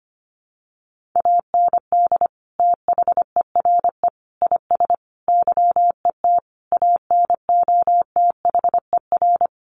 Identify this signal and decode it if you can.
Morse code, heard as a keyed tone, about 25 wpm, 700 Hz. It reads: AND THERE IS YET ANOTHER
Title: Sentences from Top 1500 Words 25wpm